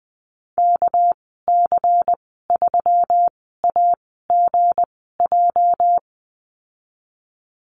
Morse code, heard as a keyed tone, about 20 wpm, spaced as for 14 wpm, 700 Hz. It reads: KC3AGJ
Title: 213 Lesson 34g - Number 8 - Call Signs 20wpm